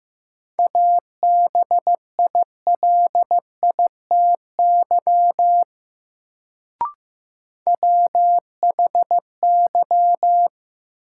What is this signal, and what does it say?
Morse code, heard as a keyed tone, about 15 wpm, 700 Hz. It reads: ABILITY WHY
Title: Sets of 3 Words - New Word Drill - 15wpm